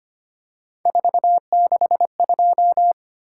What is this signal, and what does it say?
Morse code, heard as a keyed tone, about 25 wpm, 700 Hz. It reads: 462
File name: Three Letter-Number Combinations - 25wpm